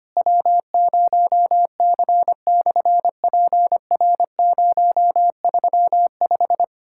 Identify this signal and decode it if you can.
Morse code, heard as a keyed tone, about 25 wpm, 700 Hz. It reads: W0C/PR035
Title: SOTA QSO Practice 25wpm